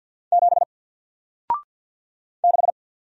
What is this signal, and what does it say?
Morse code, heard as a keyed tone, about 48 wpm, 700 Hz. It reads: 7 6
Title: Single Character (Letter:Number:Punctuation:Prosign) Alt 50wpm